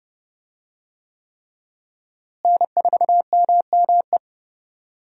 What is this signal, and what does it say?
Morse code, heard as a keyed tone, about 30 wpm, 700 Hz. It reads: N4MME
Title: USA Call Signs - 5 Characters - Repeated 3x - 30wpm